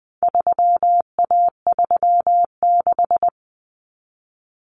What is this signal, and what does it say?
Morse code, heard as a keyed tone, about 20 wpm, 700 Hz. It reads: 3A36
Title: Common Cut Numbers - 4-Digit 20wpm